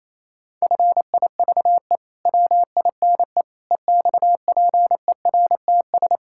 Morse code, heard as a keyed tone, about 28 wpm, 700 Hz. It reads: FIVE WINE EXPERTS
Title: Pangram Challenge v2 - 28wpm